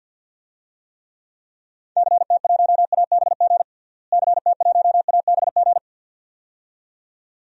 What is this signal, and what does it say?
Morse code, heard as a keyed tone, about 49 wpm, 700 Hz. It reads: CT1ABG CT1ABG
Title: DXCC-100 Call Signs - 6 Characters - Repeated 2x - 50wpm